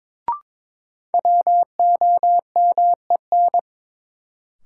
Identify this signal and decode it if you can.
Morse code, heard as a keyed tone, about 21 wpm, 700 Hz. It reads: WOMEN
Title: Sets of 3 Words - New Word Drill - 22wpm